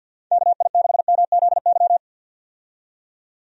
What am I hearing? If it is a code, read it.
Morse code, heard as a keyed tone, about 51 wpm, 700 Hz. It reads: KI6MZY
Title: USA Call Signs - 6 Characters - 50wpm